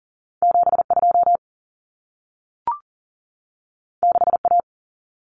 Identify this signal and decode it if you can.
Morse code, heard as a keyed tone, about 40 wpm, 700 Hz. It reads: 72 6A
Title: Common Cut Numbers - 2-Digit 40wpm